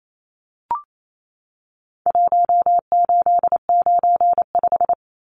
Morse code, heard as a keyed tone, about 28 wpm, 700 Hz. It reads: 1895
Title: Numbers - Contest Style - 4-Digit 28wpm